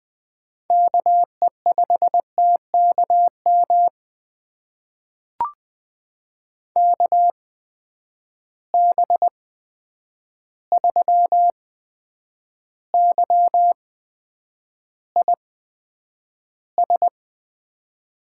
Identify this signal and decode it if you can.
Morse code, heard as a keyed tone, about 20 wpm, spaced as for 9 wpm, 700 Hz. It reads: KE5TKM KB3YIS
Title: USA Call Signs - 6 Characters - 8x Character Spacing - 20wpm